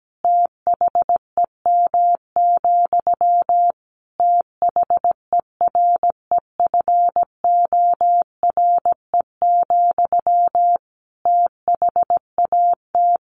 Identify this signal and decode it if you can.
Morse code, heard as a keyed tone, about 17 wpm, 700 Hz. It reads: THEM, THEREFORE, THAT
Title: Sentences from Top 800 Words 17wpm